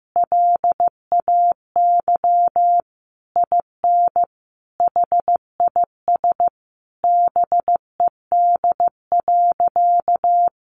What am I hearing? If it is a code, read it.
Morse code, heard as a keyed tone, about 14 wpm, 700 Hz. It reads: LAY IN HIS BED.
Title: Sentences from Top 500 Words 15wpm